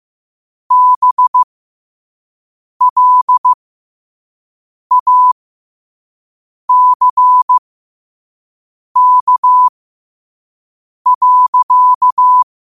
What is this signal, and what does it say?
Morse code, heard as a keyed tone, about 15 wpm, spaced as for 5 wpm, 1000 Hz. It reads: BLACK.
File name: qotdfiles / 5wpm